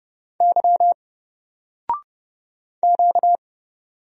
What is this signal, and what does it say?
Morse code, heard as a keyed tone, about 30 wpm, 700 Hz. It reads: Y Q
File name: Single-Letter-Number-30wpm